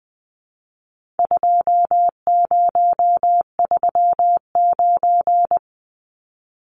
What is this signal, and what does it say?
Morse code, heard as a keyed tone, about 20 wpm, 700 Hz. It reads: 2039
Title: Numbers - 4-Digit 20wpm